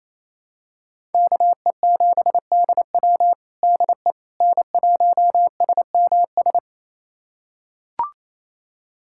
Morse code, heard as a keyed tone, about 28 wpm, 700 Hz. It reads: KE7DW DE N1SMS
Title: USA Callsign DE Callsign - 5 Characters - 28wpm